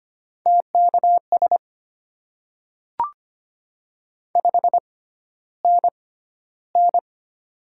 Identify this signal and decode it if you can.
Morse code, heard as a keyed tone, about 25 wpm, spaced as for 13 wpm, 690 Hz. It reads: TKS 5NN
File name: Top 90 QSO Elements - 6x Character Spacing - 25wpm